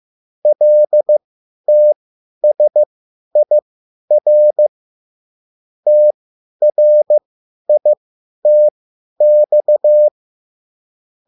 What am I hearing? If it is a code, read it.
Morse code, heard as a keyed tone, about 15 wpm, spaced as for 10 wpm, 600 Hz. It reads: LTSIR TRITX